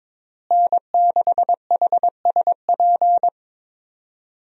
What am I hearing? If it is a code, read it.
Morse code, heard as a keyed tone, about 22 wpm, 700 Hz. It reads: N6HSP
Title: USA Call Signs - 5 Characters - Repeated 2x - 22wpm